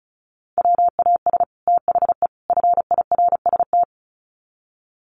Morse code, heard as a keyed tone, about 34 wpm, 700 Hz. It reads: WAS THE FIRST
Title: Sets of 3 Words 35wpm